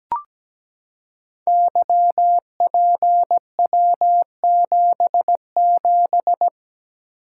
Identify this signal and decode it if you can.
Morse code, heard as a keyed tone, about 17 wpm, 700 Hz. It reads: YPW77
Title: Code Groups - 5 Letter:Numbers - 17wpm